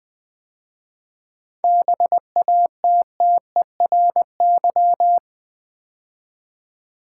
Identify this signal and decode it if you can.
Morse code, heard as a keyed tone, about 20 wpm, 700 Hz. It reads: BATTERY
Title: 7 Letter Words - First 3 Letters Given - 20wpm